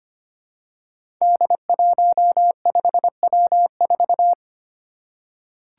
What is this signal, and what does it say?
Morse code, heard as a keyed tone, about 25 wpm, 700 Hz. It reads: D15W4
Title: Five Challenging Letter-Number Combinations 25wpm